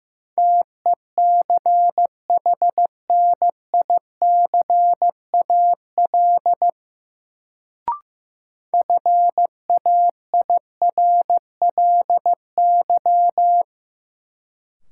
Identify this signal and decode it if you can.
Morse code, heard as a keyed tone, about 15 wpm, 700 Hz. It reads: TECHNICAL FAIRLY
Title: Sentences from Top 1100 Words - New Word Drill Alt - 15wpm